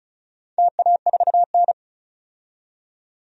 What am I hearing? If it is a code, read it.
Morse code, heard as a keyed tone, about 36 wpm, 700 Hz. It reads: TA4N
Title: DXCC-100 Call Signs - 4 Characters - 35wpm